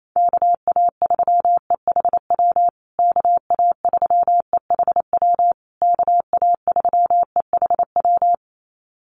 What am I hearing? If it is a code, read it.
Morse code, heard as a keyed tone, about 28 wpm, 700 Hz. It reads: KA3EHW KA3EHW KA3EHW
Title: USA Call Signs Repeated 3x 28wpm